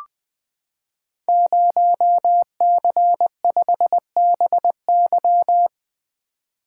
Morse code, heard as a keyed tone, about 21 wpm, 700 Hz. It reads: 0C5BY
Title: Code Groups - 5 Letter:Numbers - 20wpm